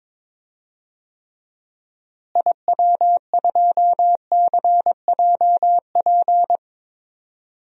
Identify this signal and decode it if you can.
Morse code, heard as a keyed tone, about 22 wpm, 700 Hz. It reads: IW2CJP
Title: DXCC-100 Call Signs - 6 Characters - Repeated 3x - 22wpm